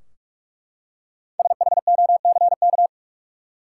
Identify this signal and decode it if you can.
Morse code, heard as a keyed tone, about 43 wpm, 700 Hz. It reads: SHOCK
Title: Five Letter Words - 45wpm